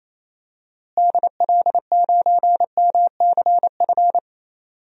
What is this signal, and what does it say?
Morse code, heard as a keyed tone, about 28 wpm, 700 Hz. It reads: DL9MCF
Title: DXCC-100 Call Signs - 6 Characters - Repeated 3x - 28wpm